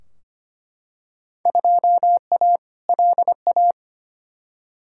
Morse code, heard as a keyed tone, about 25 wpm, 700 Hz. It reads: 2A LA
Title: Field Day Exchanges - 25wpm